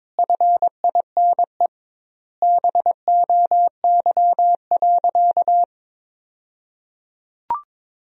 Morse code, heard as a keyed tone, about 22 wpm, 700 Hz. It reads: FINE BOY.
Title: 2x Word-Spacing - Sentences from Top 500 Words - 22wpm